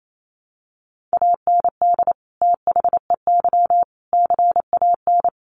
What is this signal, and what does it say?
Morse code, heard as a keyed tone, about 28 wpm, 700 Hz. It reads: AND THEY CAN
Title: Sentences from Top 1100 Words 28wpm